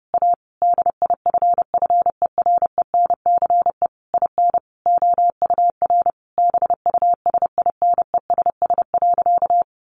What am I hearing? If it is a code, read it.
Morse code, heard as a keyed tone, about 30 wpm, 700 Hz. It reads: A DIFFERENCE IN OUR BUSINESS.
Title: Sentences from Top 1400 Words 30wpm